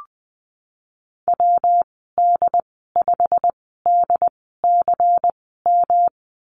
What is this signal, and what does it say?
Morse code, heard as a keyed tone, about 19 wpm, spaced as for 14 wpm, 700 Hz. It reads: WD5DCM
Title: 193 Lesson 31g - Letter v - Call Signs 20wpm